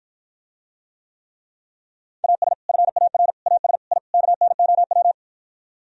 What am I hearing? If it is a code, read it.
Morse code, heard as a keyed tone, about 52 wpm, 700 Hz. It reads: AS FAR AS I KNOW